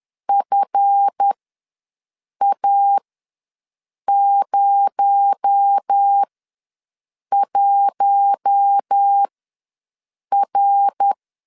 Morse code, heard as a keyed tone, about 10 wpm, spaced as for 6 wpm, 800 Hz. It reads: FA01R
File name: LEZIONI / 7 GF9Z / 7 GF ws 10wpm 3918